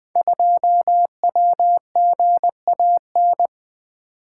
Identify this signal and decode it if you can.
Morse code, heard as a keyed tone, about 20 wpm, 700 Hz. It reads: 2WGAN
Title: Code Groups - 5 Letter:Numbers - 20wpm